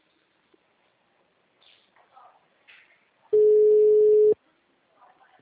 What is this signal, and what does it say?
Ringback tone (Europe, 425 Hz).